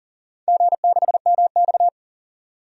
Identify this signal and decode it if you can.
Morse code, heard as a keyed tone, about 37 wpm, 700 Hz. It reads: G6MX
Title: DXCC-100 Call Signs - 4 Characters - 40wpm